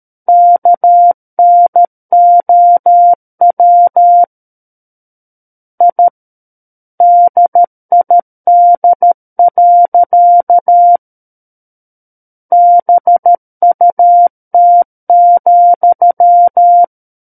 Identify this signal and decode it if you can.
Morse code, heard as a keyed tone, about 13 wpm, 700 Hz. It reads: KNOW I DID. BUT,
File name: Book One / 04-Chapter 4 - 13 WPM